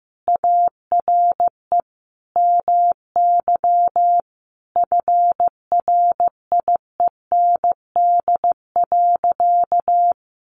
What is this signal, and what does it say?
Morse code, heard as a keyed tone, about 15 wpm, 700 Hz. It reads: ARE MY FRIEND.
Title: Sentences from Top 1000 Words 15wpm